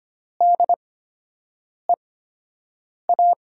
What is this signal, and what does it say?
Morse code, heard as a keyed tone, about 25 wpm, spaced as for 7 wpm, 700 Hz. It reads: DEA
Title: USA Call Signs - 5 Characters - 8x Character Spacing - 25wpm